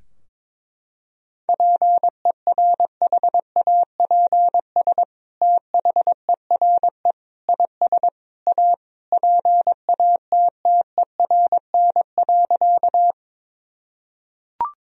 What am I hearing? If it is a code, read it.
Morse code, heard as a keyed tone, about 22 wpm, 700 Hz. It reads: PERHAPS THERE IS A PATTERN.
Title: Sentences from Top 1000 Words - Review Part 2 - 22wpm